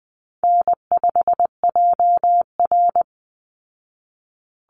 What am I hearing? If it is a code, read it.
Morse code, heard as a keyed tone, about 20 wpm, 700 Hz. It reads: N5JR
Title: Easy Call Signs Repeated 3x - 20wpm